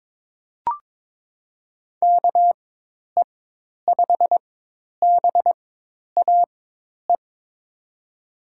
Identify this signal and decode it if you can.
Morse code, heard as a keyed tone, about 21 wpm, spaced as for 10 wpm, 700 Hz. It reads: KE5BAE
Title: USA Call Signs - 6 Characters - 4x Character Spacing - 22wpm